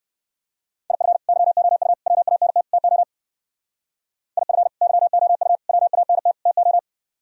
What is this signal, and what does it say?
Morse code, heard as a keyed tone, about 58 wpm, 700 Hz. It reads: IF YOU WANT TO IF YOU WANT TO